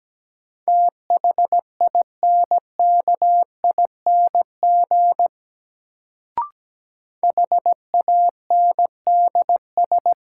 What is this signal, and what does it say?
Morse code, heard as a keyed tone, about 17 wpm, 700 Hz. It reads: THINKING HANDS
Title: Sentences from Top 1000 Words - Review Part 6 - New Word Drill - 17wpm